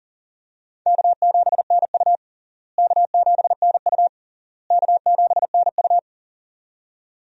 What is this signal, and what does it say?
Morse code, heard as a keyed tone, about 40 wpm, 700 Hz. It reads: K7NU K7NU K7NU
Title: USA Call Signs - 4 Characters - Repeated 3x - 40wpm